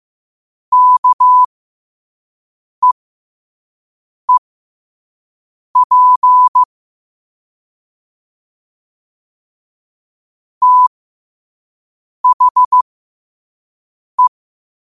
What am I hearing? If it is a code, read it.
Morse code, heard as a keyed tone, about 15 wpm, spaced as for 5 wpm, 1000 Hz. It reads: KEEP THE
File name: qotdfiles / 5wpm